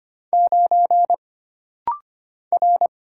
Morse code, heard as a keyed tone, about 25 wpm, 700 Hz. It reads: 9 R
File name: Single Character (Letter:Number:Punctuation:Prosign) - Rapid-Fire - 25wpm